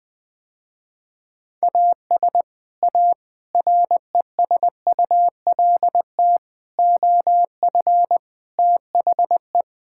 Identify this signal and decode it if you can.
Morse code, heard as a keyed tone, about 20 wpm, 700 Hz. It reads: AS A RESULT OF THE
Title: Sentences with 4-Word Phrases - Spoken Beginning - 20wpm